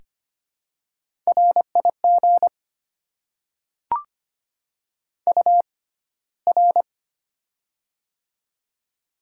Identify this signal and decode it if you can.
Morse code, heard as a keyed tone, about 25 wpm, spaced as for 15 wpm, 700 Hz. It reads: RIG UR
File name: Top 90 QSO Elements - 6x Character Spacing - 25wpm